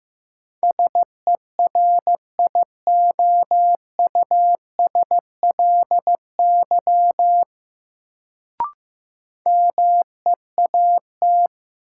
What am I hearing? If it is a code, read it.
Morse code, heard as a keyed tone, about 15 wpm, 700 Hz. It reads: SERIOUSLY MEAT
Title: Sentences from Top 1200 Words - New Word Drill Alt - 15wpm